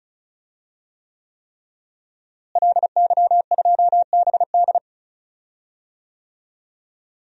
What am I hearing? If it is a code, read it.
Morse code, heard as a keyed tone, about 36 wpm, 700 Hz. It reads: LY2BD
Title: DXCC-100 Call Signs - 5 Characters - Repeated 3x - 35wpm